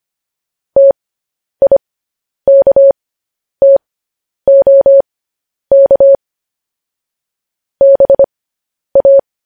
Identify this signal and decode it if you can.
Morse code, heard as a keyed tone, about 25 wpm, spaced as for 10 wpm, 550 Hz. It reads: TIKTOK BA
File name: podcast / News headlines-1x-25x10